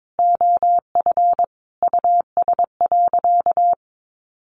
Morse code, heard as a keyed tone, about 22 wpm, 700 Hz. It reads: OF US.
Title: Sentences from Top 1300 Words 22wpm